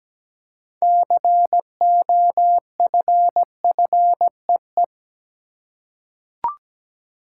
Morse code, heard as a keyed tone, about 17 wpm, 700 Hz. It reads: COFFEE
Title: Six Letter Words - 17wpm